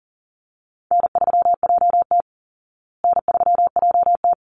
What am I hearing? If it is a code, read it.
Morse code, heard as a keyed tone, about 40 wpm, 700 Hz. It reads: N3JT N3JT
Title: Easy Call Signs Repeated 2x - 40wpm